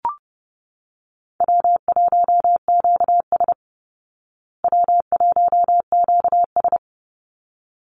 Morse code, heard as a keyed tone, about 30 wpm, 700 Hz. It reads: W1QS W1QS
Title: Easy Call Signs Repeated 2x - 30wpm